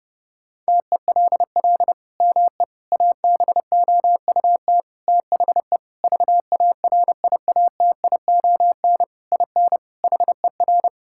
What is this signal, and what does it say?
Morse code, heard as a keyed tone, about 32 wpm, 700 Hz. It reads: TELL ME ABOUT THE VARIATION IN HER
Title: Sentences from Top 1600 Words 30wpm